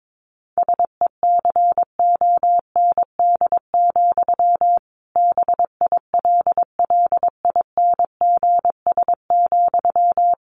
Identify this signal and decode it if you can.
Morse code, heard as a keyed tone, about 22 wpm, 700 Hz. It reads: SECOND, BILLINGS,
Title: A Sound Of Thunder 22wpm